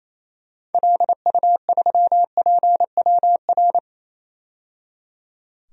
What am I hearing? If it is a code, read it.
Morse code, heard as a keyed tone, about 28 wpm, 700 Hz. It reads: LU3PWR
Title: DXCC-100 Call Signs - 6 Characters - 28wpm